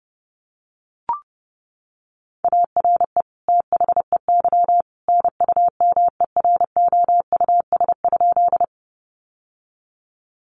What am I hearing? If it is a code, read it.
Morse code, heard as a keyed tone, about 30 wpm, 700 Hz. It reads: ARE THEY NUMEROUS?
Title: Sentences from Top 1400 Words 30wpm